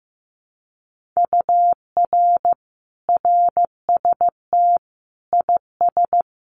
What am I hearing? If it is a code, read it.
Morse code, heard as a keyed tone, about 15 wpm, 700 Hz. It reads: UR RST IS
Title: Common QSO Phrases Encore 15wpm